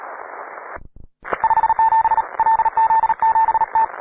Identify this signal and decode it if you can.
Morse code, heard as a keyed tone, about 39 wpm, 890 Hz, with radio noise. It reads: 57 LZ7T
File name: wae / cw pro / rt9w cw